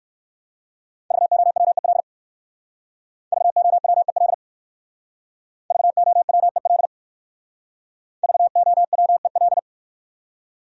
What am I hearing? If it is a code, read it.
Morse code, heard as a keyed tone, about 58 wpm, 700 Hz. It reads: VOWEL VOWEL VOWEL VOWEL